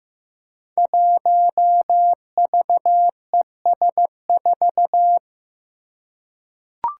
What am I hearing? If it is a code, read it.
Morse code, heard as a keyed tone, about 15 wpm, 700 Hz. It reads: 1VES4
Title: Five Challenging Letter-Number Combinations 15wpm